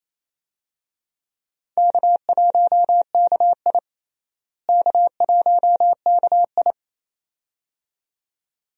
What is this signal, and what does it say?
Morse code, heard as a keyed tone, about 28 wpm, 700 Hz. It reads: K1KI K1KI
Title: USA Call Signs - 4 Characters - Repeated 2x - 28wpm